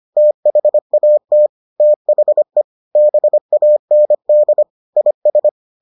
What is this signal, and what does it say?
Morse code, heard as a keyed tone, about 25 wpm, 600 Hz. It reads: THAT THE BAND IS